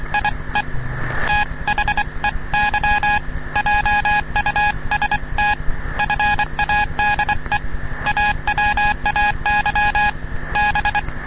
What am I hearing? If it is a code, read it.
Morse code, heard as a keyed tone, about 24 wpm, 830 Hz, with noise. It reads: IE THEY JUST FADE AWAY B
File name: europe / bel / bel-265-ostooestenderadioclg-g-240-gldlsndsend-1997-ll